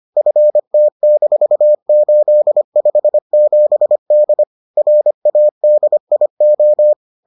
Morse code, heard as a keyed tone, about 25 wpm, 600 Hz. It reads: FT-857D RADIO